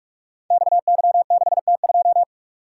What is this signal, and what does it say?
Morse code, heard as a keyed tone, about 44 wpm, 700 Hz. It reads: XY6T2
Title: Code Groups - 5 Letter:Numbers - 45wpm